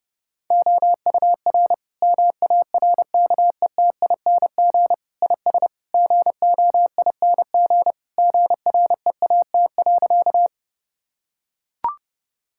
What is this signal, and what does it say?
Morse code, heard as a keyed tone, about 30 wpm, 700 Hz. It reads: OUR MARKETING IS GOING GREAT.
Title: Sentences from Top 1000 Words - Review Part 5 - 30wpm